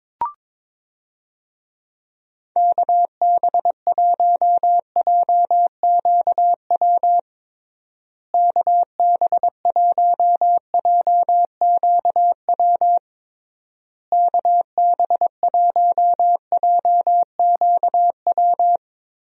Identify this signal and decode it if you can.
Morse code, heard as a keyed tone, about 22 wpm, 700 Hz. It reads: KB1JQW KB1JQW KB1JQW
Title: USA Call Signs - 6 Characters - Repeated 3x - 22wpm